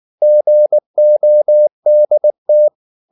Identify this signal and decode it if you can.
Morse code, heard as a keyed tone, about 19 wpm, 600 Hz. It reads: GODT
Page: Ferie 19 wpm